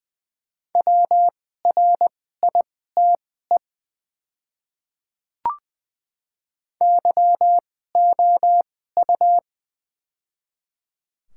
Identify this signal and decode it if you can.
Morse code, heard as a keyed tone, about 20 wpm, spaced as for 14 wpm, 690 Hz. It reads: WRITE YOU
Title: 2x Character Spacing - Top 500 Words - 20wpm